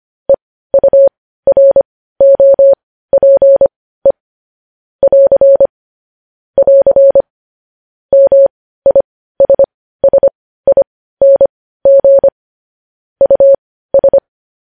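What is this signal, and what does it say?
Morse code, heard as a keyed tone, about 25 wpm, spaced as for 15 wpm, 550 Hz. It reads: EUROPE <AR> <AR> MISSING US
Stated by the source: News headlines sent in Morse code at 15 wpm.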